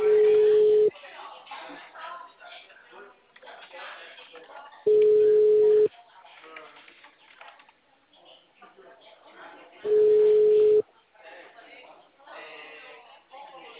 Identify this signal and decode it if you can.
Ringback tone (Europe, 425 Hz), with speech.